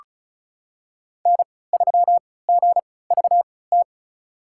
Morse code, heard as a keyed tone, about 34 wpm, spaced as for 20 wpm, 700 Hz. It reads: N3GVT
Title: USA Call Signs - 5 Characters - 3x Character Spacing - 35wpm